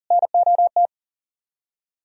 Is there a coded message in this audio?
Yes: Morse code from 0.1 to 0.9 s, heard as a keyed tone, about 51 wpm, 700 Hz. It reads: NOT